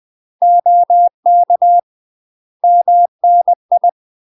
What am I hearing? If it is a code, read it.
Morse code, heard as a keyed tone, about 20 wpm, 700 Hz. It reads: OK MNI
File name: UK MASTER QSOs TRAINING / 20WPM / UK MASTER QSOs 0014